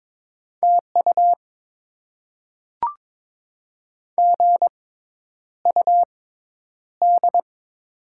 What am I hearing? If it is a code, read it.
Morse code, heard as a keyed tone, about 22 wpm, spaced as for 10 wpm, 690 Hz. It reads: TU GUD
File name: Top 20 QSO Elements - 6x Character Spacing - 22wpm